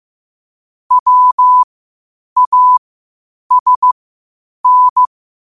Morse code, heard as a keyed tone, about 15 wpm, spaced as for 8 wpm, 1000 Hz. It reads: WASN
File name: qotdfiles / 8wpm